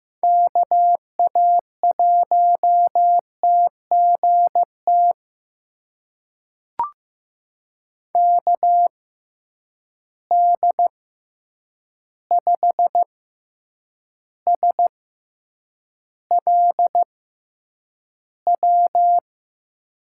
Morse code, heard as a keyed tone, about 15 wpm, spaced as for 8 wpm, 700 Hz. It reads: KA1TGT KD5SLW
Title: USA Call Signs - 6 Characters - 6x Character Spacing - 15wpm